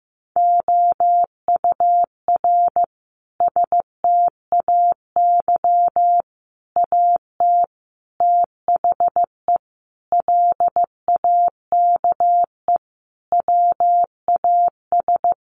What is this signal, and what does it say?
Morse code, heard as a keyed tone, about 15 wpm, 700 Hz. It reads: OUR STAY AT THE LAKE WAS
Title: Sentences from Top 1400 Words 15wpm